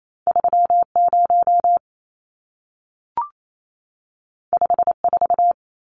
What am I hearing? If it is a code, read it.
Morse code, heard as a keyed tone, about 28 wpm, 700 Hz. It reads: 30 54
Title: Numbers - 2-Digit 28wpm